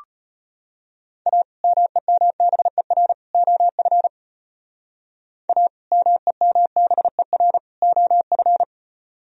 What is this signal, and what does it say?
Morse code, heard as a keyed tone, about 33 wpm, 700 Hz. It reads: A MEMBER OF A MEMBER OF